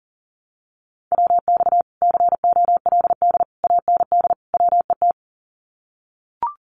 Morse code, heard as a keyed tone, about 40 wpm, 700 Hz. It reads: WX COLD AND WET
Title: Common QSO Phrases Encore 40wpm